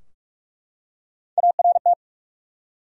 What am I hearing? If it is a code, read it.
Morse code, heard as a keyed tone, about 43 wpm, 700 Hz. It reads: ART
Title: USA Common Names - Max 3 Letters - 45wpm